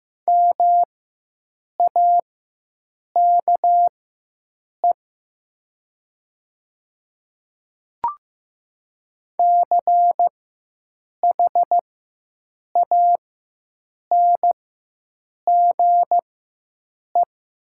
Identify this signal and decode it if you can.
Morse code, heard as a keyed tone, about 15 wpm, spaced as for 7 wpm, 700 Hz. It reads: MAKE CHANGE
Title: 4x Character Spacing - Top 200 Words - 15wpm